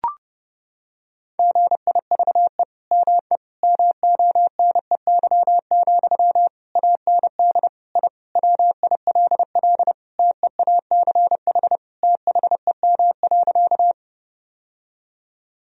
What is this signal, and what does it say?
Morse code, heard as a keyed tone, about 32 wpm, 700 Hz. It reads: GIVE ME MONEY, AND I WILL TEACH THEM.
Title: Sentences from Top 1000 Words - Review Part 2 - 30wpm